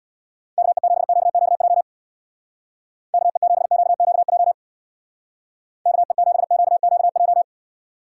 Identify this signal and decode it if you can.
Morse code, heard as a keyed tone, about 62 wpm, 700 Hz. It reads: KE7QQJ KE7QQJ KE7QQJ
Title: Speed Racing - USA 2x3 Call Signs 50wpm